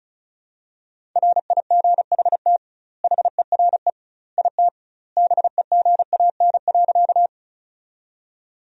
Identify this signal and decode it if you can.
Morse code, heard as a keyed tone, about 36 wpm, 700 Hz. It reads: RIGHT HERE IT BEGAN.
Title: 2x Word-Spacing - Sentences from Top 300 Words - 35wpm